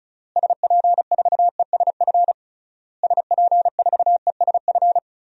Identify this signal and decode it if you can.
Morse code, heard as a keyed tone, about 36 wpm, 700 Hz. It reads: SP4ESF SP4ESF
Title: DXCC-100 Call Signs - 6 Characters - Repeated 3x - 35wpm